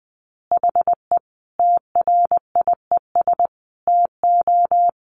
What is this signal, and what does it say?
Morse code, heard as a keyed tone, about 20 wpm, 700 Hz. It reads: HE TRIES TO
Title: Additional Easy Phrases 20wpm